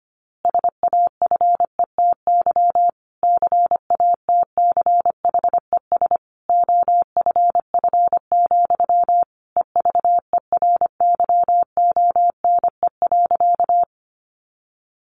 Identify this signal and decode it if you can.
Morse code, heard as a keyed tone, about 25 wpm, 700 Hz. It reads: SAFETY CATCHES OFF, EVERYONE.
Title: A Sound Of Thunder 25wpm